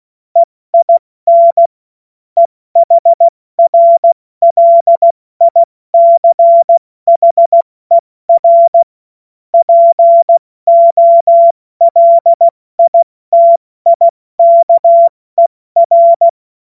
Morse code, heard as a keyed tone, about 16 wpm, 680 Hz. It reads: EIN EHRLICHER POLITIKER
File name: Morsekurs / Witze / Witz 18. April 70 F80